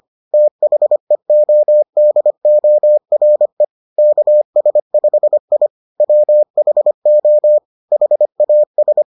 Morse code, heard as a keyed tone, about 25 wpm, 600 Hz. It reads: THEODORE KS5I WHO HAS